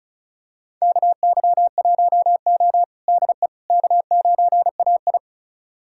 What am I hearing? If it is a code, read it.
Morse code, heard as a keyed tone, about 36 wpm, 700 Hz. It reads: KY1O DE K9AI
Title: USA Callsign DE Callsign - 4 Characters - 35wpm